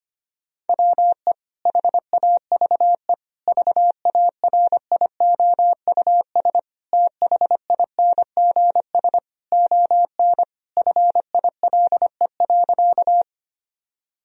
Morse code, heard as a keyed tone, about 25 wpm, 700 Hz. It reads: WE HAVE VARIOUS THINGS ON FILE.
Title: Sentences from Top 1000 Words - Review Part 4 - 25wpm